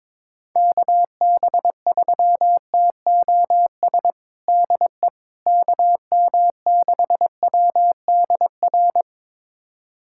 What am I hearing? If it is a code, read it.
Morse code, heard as a keyed tone, about 21 wpm, 700 Hz. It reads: KB3TOS DE KM6WDR
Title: USA Callsign DE Callsign - 6 Characters - 22wpm